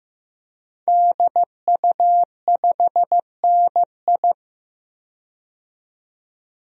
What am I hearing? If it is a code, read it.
Morse code, heard as a keyed tone, about 15 wpm, 700 Hz. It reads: DU5NI
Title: DXCC-100 Call Signs - 5 Characters - 15wpm